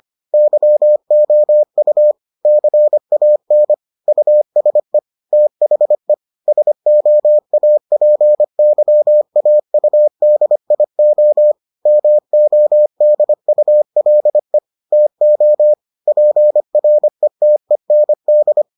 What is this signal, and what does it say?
Morse code, heard as a keyed tone, about 25 wpm, 600 Hz. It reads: YOU CAN USE THE SOAPYAUDIO MODULE TO PRETEND